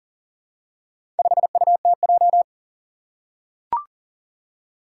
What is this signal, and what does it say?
Morse code, heard as a keyed tone, about 40 wpm, 700 Hz. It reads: 5UTJ
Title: Four Challenging Letter-Number Combinations 40wpm